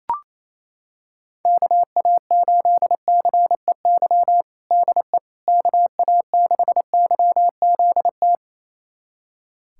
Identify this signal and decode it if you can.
Morse code, heard as a keyed tone, about 28 wpm, 700 Hz. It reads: KA8CEY DE KA6YZT
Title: USA Callsign DE Callsign - 6 Characters - 28wpm